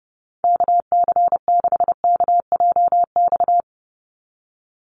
Morse code, heard as a keyed tone, about 30 wpm, 700 Hz. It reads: KC6KJX
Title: USA Call Signs 30wpm